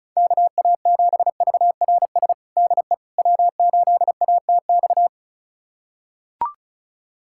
Morse code, heard as a keyed tone, about 36 wpm, 690 Hz. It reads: KA7VRS DE W8ATX
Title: USA Callsign DE Callsign 35wpm